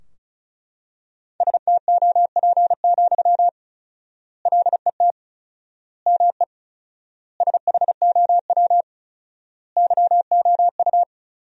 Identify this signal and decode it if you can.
Morse code, heard as a keyed tone, about 36 wpm, 700 Hz. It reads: STOP, LET ME SHOW YOU
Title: 4x Word-Spacing - Sentences from Top 300 Words - 35wpm